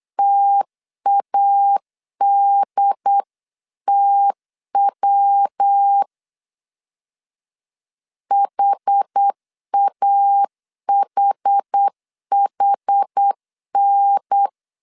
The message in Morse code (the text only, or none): TADTW HAHHN